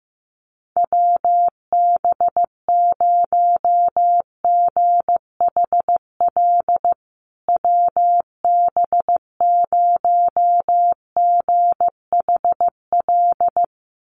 Morse code, heard as a keyed tone, about 15 wpm, 700 Hz. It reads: WB0GHL WB0GHL
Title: USA Call Signs Repeated 2x 15wpm